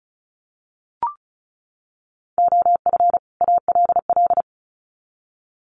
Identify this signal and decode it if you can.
Morse code, heard as a keyed tone, about 34 wpm, 700 Hz. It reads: OF ALL
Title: Sets of 2 Words 35wpm